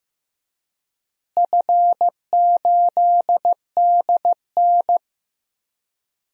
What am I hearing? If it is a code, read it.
Morse code, heard as a keyed tone, about 15 wpm, 700 Hz. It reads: F8DN
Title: DXCC-100 Call Signs - 4 Characters - Repeated 2x - 15wpm